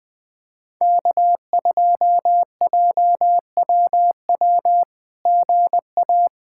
Morse code, heard as a keyed tone, about 20 wpm, 700 Hz. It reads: K2JWW GA
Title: SOTA QSO Practice 20wpm